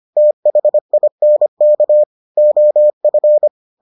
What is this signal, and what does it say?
Morse code, heard as a keyed tone, about 25 wpm, 600 Hz. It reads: THINK OF